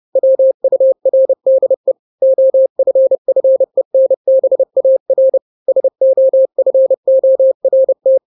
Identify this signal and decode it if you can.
Morse code, heard as a keyed tone, about 30 wpm, 530 Hz. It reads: WURDE OFFENBAR SOFORT
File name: 29wpm